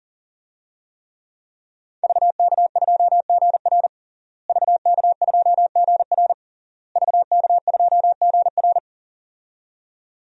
Morse code, heard as a keyed tone, about 40 wpm, 700 Hz. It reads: VK2GR VK2GR VK2GR
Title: DXCC-100 Call Signs - 5 Characters - Repeated 3x - 40wpm